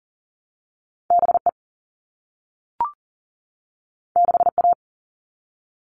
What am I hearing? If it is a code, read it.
Morse code, heard as a keyed tone, about 38 wpm, 700 Hz. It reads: BE 6A
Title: All Cut Numbers - 2-Digit 40wpm